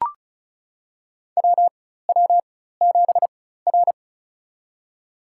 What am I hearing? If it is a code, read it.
Morse code, heard as a keyed tone, about 36 wpm, spaced as for 16 wpm, 700 Hz. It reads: WW7R
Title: USA Call Signs - 4 Characters - 4x Character Spacing - 35wpm